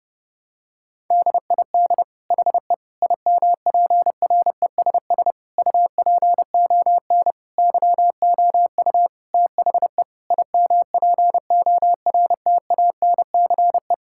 Morse code, heard as a keyed tone, about 32 wpm, 700 Hz. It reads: DID HE IMPRESS UPON YOU THE IMPORTANCE
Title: Sentences from Top 1600 Words 30wpm